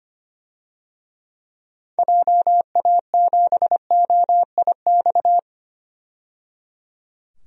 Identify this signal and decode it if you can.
Morse code, heard as a keyed tone, about 25 wpm, 700 Hz. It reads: JA7OIX
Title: DXCC-100 Call Signs - 6 Characters - 25wpm